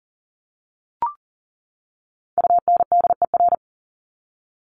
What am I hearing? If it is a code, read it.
Morse code, heard as a keyed tone, about 40 wpm, 700 Hz. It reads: UNDER
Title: Top 500 Words 40wpm